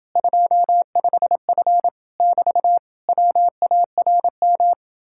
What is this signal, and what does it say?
Morse code, heard as a keyed tone, about 30 wpm, 700 Hz. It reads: 25F <BT> WARM